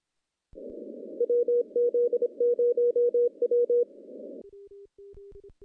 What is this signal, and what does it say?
Morse code, heard as a keyed tone, about 26 wpm, 480 Hz. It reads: WZ0W
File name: contests / CW-SS-k5tr- -1400